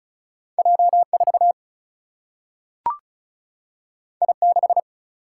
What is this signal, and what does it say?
Morse code, heard as a keyed tone, about 36 wpm, 700 Hz. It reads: J4 I6
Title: Single-Letter Single-Number 35wpm